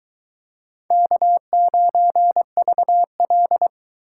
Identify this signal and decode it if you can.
Morse code, heard as a keyed tone, about 22 wpm, 700 Hz. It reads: K9VL